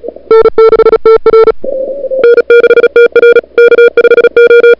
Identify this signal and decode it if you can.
Morse code, heard as a keyed tone, about 36 wpm, 440 Hz. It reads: N6TR N6TR K5O
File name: n6tr / Fab2007CWSprintHour1